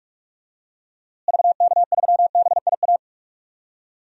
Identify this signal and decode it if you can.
Morse code, heard as a keyed tone, about 43 wpm, 700 Hz. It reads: VK3BIA
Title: DXCC-100 Call Signs - 6 Characters - Repeated 2x - 45wpm